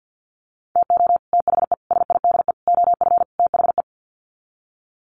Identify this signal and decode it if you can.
Morse code, heard as a keyed tone, about 50 wpm, 700 Hz. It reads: TO THE SIDE OF THE
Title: Sets of 5 Words 50wpm